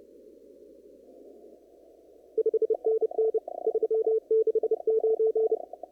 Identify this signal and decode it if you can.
Morse code, heard as a keyed tone, about 30 wpm, 440 Hz. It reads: 5NN 369
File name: CW / kt5j wpx cw-op-k5pi